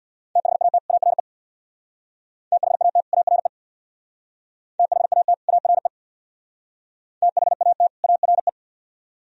Morse code, heard as a keyed tone, about 60 wpm, 700 Hz. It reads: THAT ARE THAT ARE THAT ARE THAT ARE
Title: Speed-Racing - Sets of 2 Words 50wpm